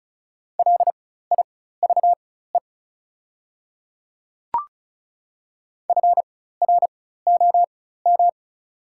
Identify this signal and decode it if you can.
Morse code, heard as a keyed tone, about 34 wpm, spaced as for 16 wpm, 690 Hz. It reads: LIVE FROM
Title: 4x Character Spacing - Top 200 Words - 35wpm